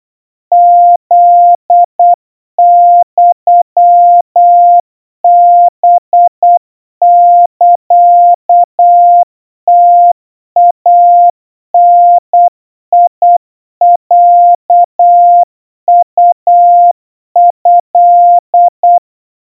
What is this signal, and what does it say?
Morse code (Japanese Wabun code), heard as a keyed tone, about 8 wpm, 700 Hz. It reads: フユハサムイダロウト
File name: 30 cpm 700 Hz Crispy